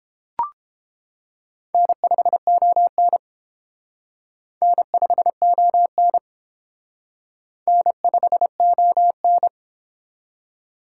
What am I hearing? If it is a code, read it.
Morse code, heard as a keyed tone, about 28 wpm, 700 Hz. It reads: N5ON N5ON N5ON